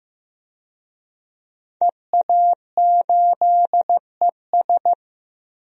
Morse code, heard as a keyed tone, about 15 wpm, 700 Hz. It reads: EA8ES
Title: DXCC-100 Call Signs - 5 Characters - 15wpm